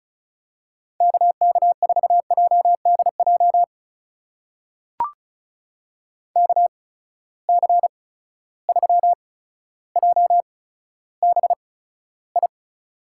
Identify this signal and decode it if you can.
Morse code, heard as a keyed tone, about 34 wpm, spaced as for 15 wpm, 700 Hz. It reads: KK4JDJ KC3JBI
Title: USA Call Signs - 6 Characters - 8x Character Spacing - 35wpm